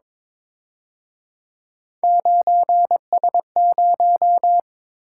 Morse code, heard as a keyed tone, about 28 wpm, 700 Hz. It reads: 9S0
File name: Speed-Racing - Three Letter-Number Combinations - 22wpm